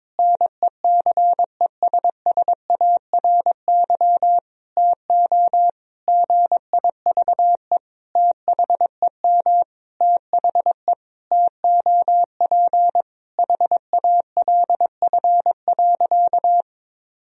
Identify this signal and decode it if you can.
Morse code, heard as a keyed tone, about 22 wpm, 700 Hz. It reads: NECESSARY TO GIVE THEM THE TOP HALF.
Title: Sentences from Top 1000 Words - Review Part 2 - 22wpm